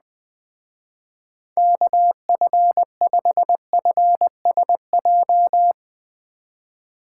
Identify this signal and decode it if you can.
Morse code, heard as a keyed tone, about 20 wpm, 700 Hz. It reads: KF5FSJ
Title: Speed Racing - USA 2x3 Call Signs 17wpm